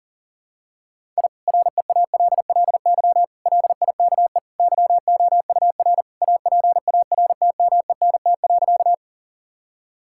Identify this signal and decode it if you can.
Morse code, heard as a keyed tone, about 40 wpm, 700 Hz. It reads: I REALLY LIKE YOUR APARTMENT.
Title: Sentences from Top 1000 Words - Review Part 6 - 40wpm